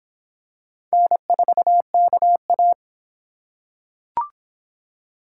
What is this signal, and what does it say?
Morse code, heard as a keyed tone, about 26 wpm, 700 Hz. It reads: N4KA